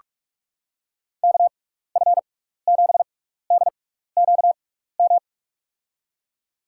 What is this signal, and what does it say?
Morse code, heard as a keyed tone, about 46 wpm, spaced as for 15 wpm, 700 Hz. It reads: KF7DQM
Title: USA Call Signs - 6 Characters - 6x Character Spacing - 45wpm